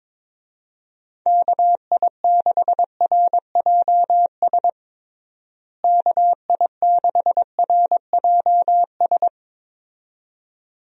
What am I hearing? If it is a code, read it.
Morse code, heard as a keyed tone, about 22 wpm, 700 Hz. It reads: KI6RJS KI6RJS
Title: USA Call Signs - 6 Characters - Repeated 2x - 22wpm